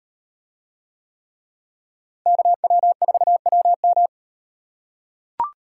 Morse code, heard as a keyed tone, about 38 wpm, 700 Hz. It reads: KW4WM